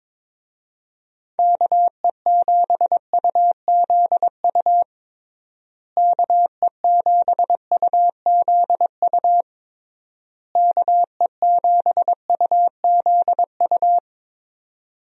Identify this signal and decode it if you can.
Morse code, heard as a keyed tone, about 22 wpm, 700 Hz. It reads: KE7UZU KE7UZU KE7UZU
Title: USA Call Signs - 6 Characters - Repeated 3x - 22wpm